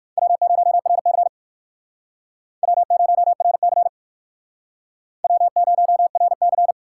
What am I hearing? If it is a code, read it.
Morse code, heard as a keyed tone, about 56 wpm, 700 Hz. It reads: W0RC W0RC W0RC
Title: Speed Racing - USA 1x2 Call Signs 45wpm